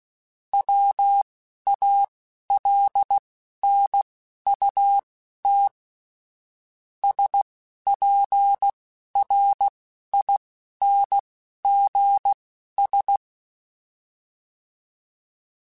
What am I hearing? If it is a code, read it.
Morse code, heard as a keyed tone, about 16 wpm, spaced as for 11 wpm, 800 Hz. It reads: WALNUT SPRINGS